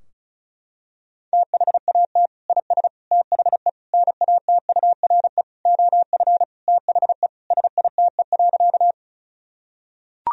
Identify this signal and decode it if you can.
Morse code, heard as a keyed tone, about 36 wpm, 700 Hz. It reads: THAT IS THE NATURE OF THE SITE.
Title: Sentences from Top 1000 Words - Review Part 4 - 35wpm